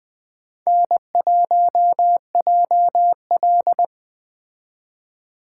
Morse code, heard as a keyed tone, about 20 wpm, 700 Hz. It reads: N1JL
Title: Speed Racing - USA 1x2 Call Signs 17wpm